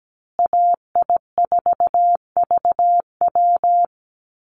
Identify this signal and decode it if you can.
Morse code, heard as a keyed tone, about 17 wpm, 700 Hz. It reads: AI4VW
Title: USA Call Signs Repeated 3x 17wpm